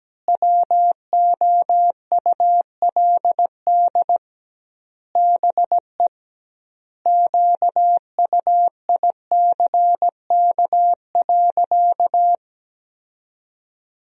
Morse code, heard as a keyed tone, about 17 wpm, 700 Hz. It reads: WOULD BE QUICK.
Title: 2x Word-Spacing - Sentences from Top 500 Words - 17wpm